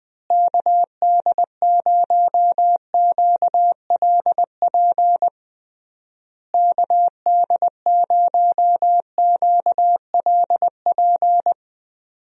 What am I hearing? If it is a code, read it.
Morse code, heard as a keyed tone, about 21 wpm, 700 Hz. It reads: KD0QLP KD0QLP
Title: USA Call Signs - 6 Characters - Repeated 3x - 20wpm